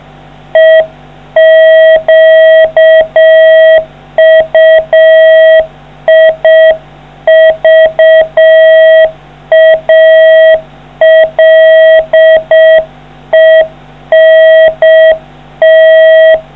Morse code, heard as a keyed tone, about 6 wpm, 660 Hz. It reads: EQUIVALENT